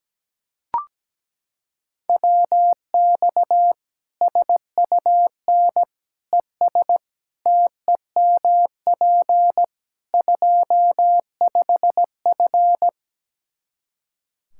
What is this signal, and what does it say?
Morse code, heard as a keyed tone, about 17 wpm, 700 Hz. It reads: WX SUN ES TEMP 25F
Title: Practice QSO - Exchange 2 - Info - 17wpm